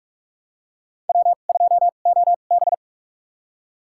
Morse code, heard as a keyed tone, about 46 wpm, spaced as for 32 wpm, 700 Hz. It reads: W2OB
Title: USA Call Signs - 4 Characters - 2x Character Spacing - 45wpm